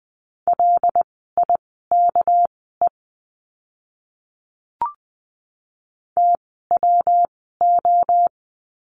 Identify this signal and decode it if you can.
Morse code, heard as a keyed tone, about 20 wpm, spaced as for 15 wpm, 700 Hz. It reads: LIKE TWO
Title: 244 Lesson 39f - Letter z - Words 20wpm